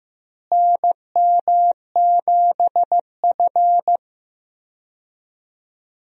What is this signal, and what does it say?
Morse code, heard as a keyed tone, about 15 wpm, 700 Hz. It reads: NM7F
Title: USA Call Signs - 4 Characters - Repeated 3x - 15wpm